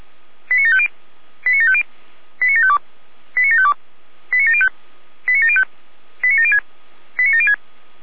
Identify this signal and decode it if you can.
Five-tone selective call (ZVEI), heard as quick runs of five tones, tones of about 70 ms. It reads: 89750 89750 89752 89752 89705 89705 89706 89706
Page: /455/SDR/ZVEI/